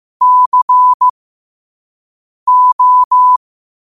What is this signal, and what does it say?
Morse code, heard as a keyed tone, about 15 wpm, spaced as for 5 wpm, 1000 Hz. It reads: CO
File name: qotdfiles / 5wpm